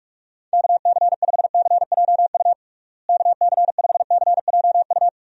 Morse code, heard as a keyed tone, about 46 wpm, 700 Hz. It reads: KC5CJU KC5CJU
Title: USA Call Signs - 6 Characters - Repeated 3x - 45wpm